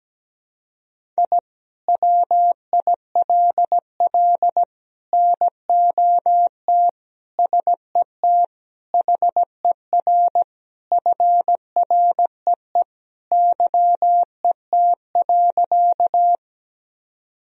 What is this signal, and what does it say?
Morse code, heard as a keyed tone, about 17 wpm, 700 Hz. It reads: I WILL NOT SET HER FREE YET.
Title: Sentences from Top 1000 Words - Review Part 1 - 17wpm